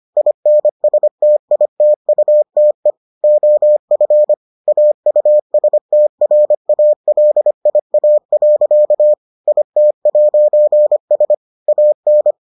Morse code, heard as a keyed tone, about 25 wpm, 600 Hz. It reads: INSTITUTE OF AUSTRALIA. IT'S AN